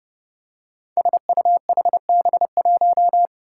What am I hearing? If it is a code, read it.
Morse code, heard as a keyed tone, about 28 wpm, 700 Hz. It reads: SUHB1
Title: Five Challenging Letter-Number Combinations 30wpm